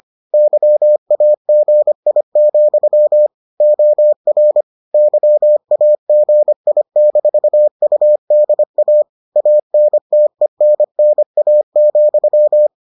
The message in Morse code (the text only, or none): YAGI, OR YAGI-UDA ANTENNA,